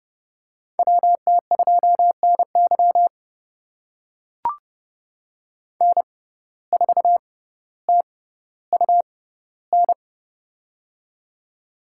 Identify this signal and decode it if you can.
Morse code, heard as a keyed tone, about 30 wpm, spaced as for 15 wpm, 700 Hz. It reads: WT2NY N4TUN
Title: USA Call Signs - 5 Characters - 6x Character Spacing - 30wpm